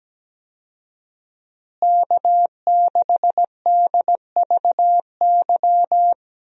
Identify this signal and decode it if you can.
Morse code, heard as a keyed tone, about 17 wpm, 700 Hz. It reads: K6DVY
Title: USA Call Signs - 5 Characters - Repeated 3x - 17wpm